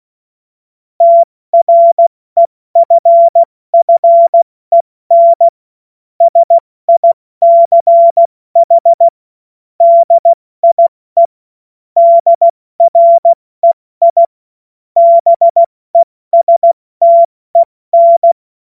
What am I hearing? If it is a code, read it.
Morse code, heard as a keyed tone, about 16 wpm, 680 Hz. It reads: TREFFEN SICH DIE DREI BESTEN
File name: Morsekurs / Witze / Witz 02 April 2024 70 F80